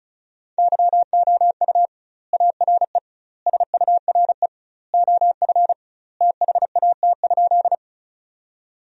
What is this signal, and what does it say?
Morse code, heard as a keyed tone, about 34 wpm, 700 Hz. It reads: YOU ARE SURE OF THAT?
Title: 2x Word-Spacing - Sentences from Top 300 Words - 35wpm